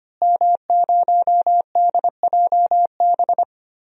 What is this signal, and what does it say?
Morse code, heard as a keyed tone, about 25 wpm, 700 Hz. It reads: M0DJB
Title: DXCC-100 Call Signs - 5 Characters - Repeated 3x - 25wpm